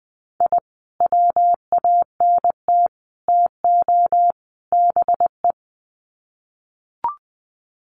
Morse code, heard as a keyed tone, about 20 wpm, 700 Hz. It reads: I WANT TO BE
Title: Sets of 4 Words 20wpm